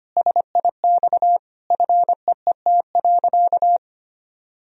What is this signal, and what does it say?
Morse code, heard as a keyed tone, about 25 wpm, 700 Hz. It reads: SIX FEET.
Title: Sentences from Top 1000 Words - Review Part 2 - 25wpm